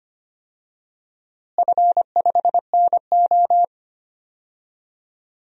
Morse code, heard as a keyed tone, about 25 wpm, 700 Hz. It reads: F5NO
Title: DXCC-100 Call Signs - 4 Characters - Repeated 2x - 25wpm